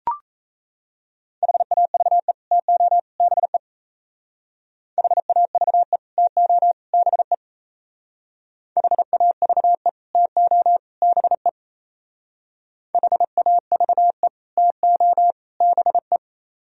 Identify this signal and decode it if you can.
Morse code, heard as a keyed tone, about 35 wpm, 700 Hz. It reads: HAVE TO BE HAVE TO BE HAVE TO BE HAVE TO BE
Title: Speed-Racing - Sets of 3 Words 28wpm